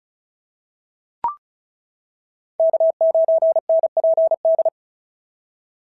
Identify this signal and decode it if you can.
Morse code, heard as a keyed tone, about 36 wpm, 690 Hz. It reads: K9NPD
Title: SOTA QSO Practice 35wpm